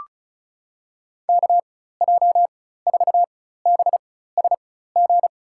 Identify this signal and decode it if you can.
Morse code, heard as a keyed tone, about 36 wpm, spaced as for 16 wpm, 700 Hz. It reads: KJ4BSG
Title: USA Call Signs - 4x Character Spacing - 35wpm